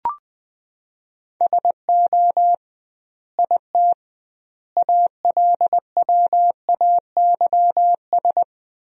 Morse code, heard as a keyed tone, about 21 wpm, 700 Hz. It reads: SO IT ALWAYS
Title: 2x Word-Spacing - Sentences from Top 300 Words - 20wpm